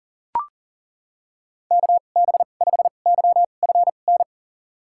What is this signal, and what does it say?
Morse code, heard as a keyed tone, about 40 wpm, spaced as for 29 wpm, 700 Hz. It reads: KB5YFN
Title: USA Call Signs - 6 Characters - 2x Character Spacing - 40wpm